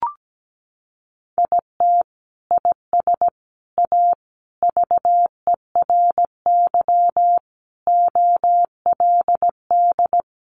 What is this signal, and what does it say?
Morse code, heard as a keyed tone, about 17 wpm, 700 Hz. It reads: IT IS A VERY OLD
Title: Sentences from Top 1200 Words 17wpm